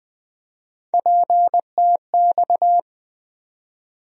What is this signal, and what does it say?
Morse code, heard as a keyed tone, about 20 wpm, 700 Hz. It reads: PTX
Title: USA Call Sign Postfixes - 3 Letters 20wpm